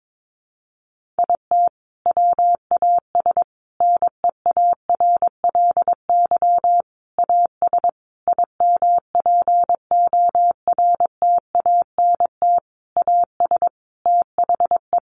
Morse code, heard as a keyed tone, about 22 wpm, 700 Hz. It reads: IT WAS NEARLY AS IMPORTANT AS THE
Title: Sentences from Top 1400 Words 22wpm